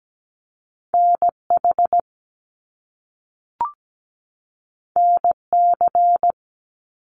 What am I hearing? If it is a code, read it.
Morse code, heard as a keyed tone, about 17 wpm, 700 Hz. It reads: NH NC
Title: US States 17wpm